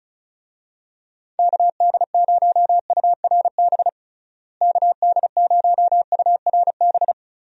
Morse code, heard as a keyed tone, about 34 wpm, 700 Hz. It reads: KD0URB KD0URB
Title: USA Call Signs - 6 Characters - Repeated 3x - 35wpm